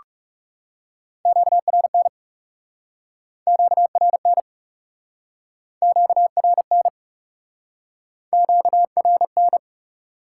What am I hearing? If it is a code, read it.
Morse code, heard as a keyed tone, about 40 wpm, 700 Hz. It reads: QRN QRN QRN QRN